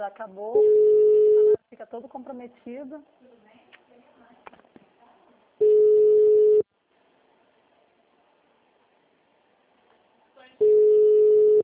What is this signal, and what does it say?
Ringback tone (Europe, 425 Hz), with speech.
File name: out- -2001- - - .2